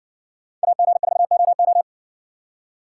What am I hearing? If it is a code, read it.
Morse code, heard as a keyed tone, about 60 wpm, 700 Hz. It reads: AG4OO